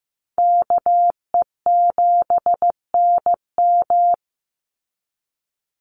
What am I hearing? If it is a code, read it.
Morse code, heard as a keyed tone, about 15 wpm, 700 Hz. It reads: KE7NM
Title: USA Call Signs 15wpm